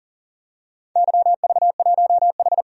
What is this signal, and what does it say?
Morse code, heard as a keyed tone, about 41 wpm, 700 Hz. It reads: YV1H
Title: DXCC-100 Call Signs - 4 Characters - 40wpm